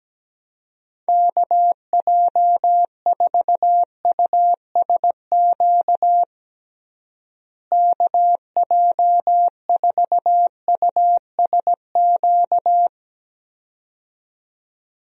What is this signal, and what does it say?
Morse code, heard as a keyed tone, about 17 wpm, 700 Hz. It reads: KJ4USQ KJ4USQ
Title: USA Call Signs - 6 Characters - Repeated 2x - 17wpm